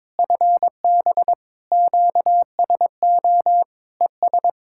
Morse code, heard as a keyed tone, about 22 wpm, 700 Hz. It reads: FB QSO ES
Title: Practice QSO - Exchange 4 - Ending - 22wpm